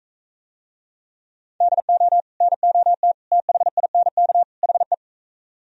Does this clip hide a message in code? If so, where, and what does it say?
Yes: Morse code from 1.6 to 4.9 s, heard as a keyed tone, about 44 wpm, 700 Hz. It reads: DO NOT THINK HE